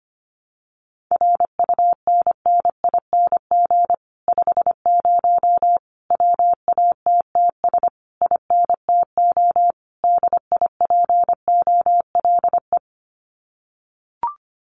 Morse code, heard as a keyed tone, about 25 wpm, 700 Hz. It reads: RUNNING 50 WATTS INTO DIPOLE
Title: Common QSO Phrases Encore 25wpm